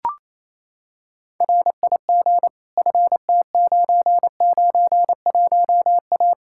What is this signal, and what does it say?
Morse code, heard as a keyed tone, about 28 wpm, 700 Hz. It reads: RIG FT991A
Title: Practice QSO - Exchange 2 - Info - 28wpm